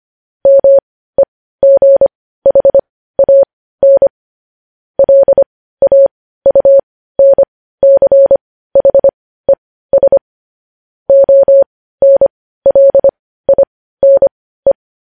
Morse code, heard as a keyed tone, about 25 wpm, spaced as for 15 wpm, 550 Hz. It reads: MEGHAN LAUNCHES ONLINE
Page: News Headlines in Morse Code at 15 WPM